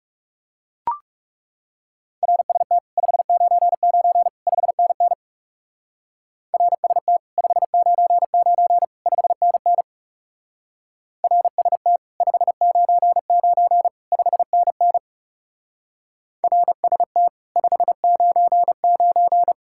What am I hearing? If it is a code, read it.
Morse code, heard as a keyed tone, about 37 wpm, 700 Hz. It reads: RST 599 5NN RST 599 5NN RST 599 5NN RST 599
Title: Speed-Racing Common QSO Phrases Encore - 30wpm